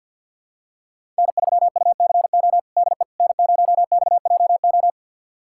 Morse code, heard as a keyed tone, about 46 wpm, 690 Hz. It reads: N3UCO DE N0XJO
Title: USA Callsign DE Callsign - 5 Characters - 50wpm